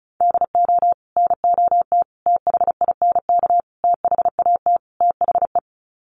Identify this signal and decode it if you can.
Morse code, heard as a keyed tone, about 34 wpm, 700 Hz. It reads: DO NOT THINK THAT THE
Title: Sets of 5 Words Encore2 35wpm